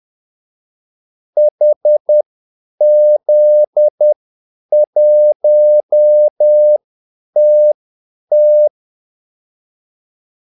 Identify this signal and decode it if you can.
Morse code, heard as a keyed tone, about 10 wpm, spaced as for 8 wpm, 600 Hz. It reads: HZ1TT
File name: Callsigns-10-8wpm